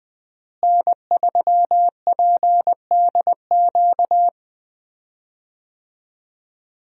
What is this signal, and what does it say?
Morse code, heard as a keyed tone, about 21 wpm, 700 Hz. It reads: N3PDQ
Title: USA Call Signs - 5 Characters - Repeated 2x - 20wpm